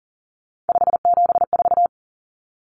Morse code, heard as a keyed tone, about 40 wpm, 700 Hz. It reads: 574
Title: Common Cut Numbers - 3-Digit 40wpm